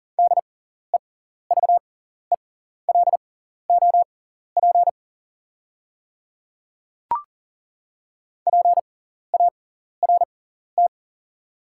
Morse code, heard as a keyed tone, about 38 wpm, spaced as for 14 wpm, 700 Hz. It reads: DEVELOP PART
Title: 6x Character Spacing - Top 500 Words - 40wpm